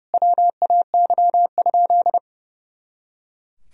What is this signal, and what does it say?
Morse code, heard as a keyed tone, about 30 wpm, 700 Hz. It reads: WAY?
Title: Sentences from Top 200 Words and Abbreviations 30wpm